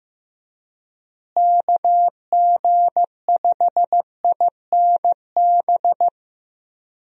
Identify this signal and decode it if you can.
Morse code, heard as a keyed tone, about 15 wpm, 700 Hz. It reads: KG5INB
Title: USA Call Signs - 6 Characters - Repeated 2x - 15wpm